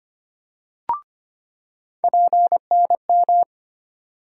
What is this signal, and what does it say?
Morse code, heard as a keyed tone, about 25 wpm, 700 Hz. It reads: PNM
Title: USA Call Sign Postfixes - 3 Letters 25wpm